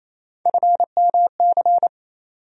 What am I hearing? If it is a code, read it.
Morse code, heard as a keyed tone, about 28 wpm, 700 Hz. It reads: FMC
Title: USA Call Sign Postfixes - 3 Letters 28wpm